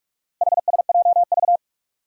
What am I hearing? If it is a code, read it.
Morse code, heard as a keyed tone, about 46 wpm, 700 Hz. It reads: HSJV
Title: Four Challenging Letter-Number Combinations 45wpm